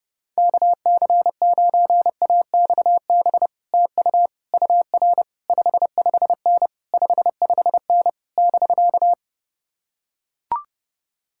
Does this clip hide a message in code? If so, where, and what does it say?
Yes: Morse code from 0.4 to 9.1 s, heard as a keyed tone, about 30 wpm, 700 Hz. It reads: KC9AXB TU UR 55N 55N <BK>